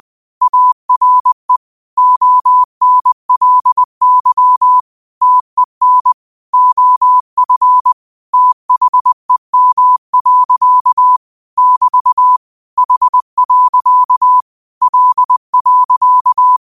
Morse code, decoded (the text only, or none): ARE ONLY TEN OF THEM. <BT> H. L.